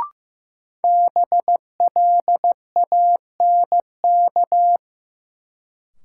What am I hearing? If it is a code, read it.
Morse code, heard as a keyed tone, about 15 wpm, 700 Hz. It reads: BLANK
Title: Sentences from Top 1500 Words - New Word Drill - 15wpm